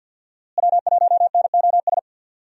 Morse code, heard as a keyed tone, about 45 wpm, 700 Hz. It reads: W1NOS
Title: USA Call Signs - 5 Characters - Repeated 2x - 50wpm